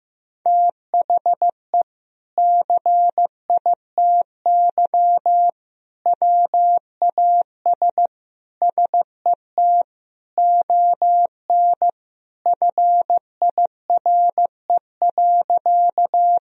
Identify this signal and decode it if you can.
Morse code, heard as a keyed tone, about 15 wpm, 700 Hz. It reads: THE CITY WAS SET ON FIRE.
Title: Sentences from Top 1000 Words - Review Part 1 - 15wpm